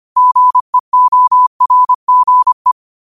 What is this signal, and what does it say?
Morse code, heard as a keyed tone, about 25 wpm, 1000 Hz. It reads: GEORGE
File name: qotdfiles / 25wpm